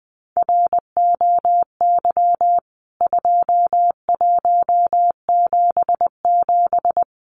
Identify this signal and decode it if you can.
Morse code, heard as a keyed tone, about 20 wpm, 700 Hz. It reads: ROY 2177
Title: CWT Exchange 20wpm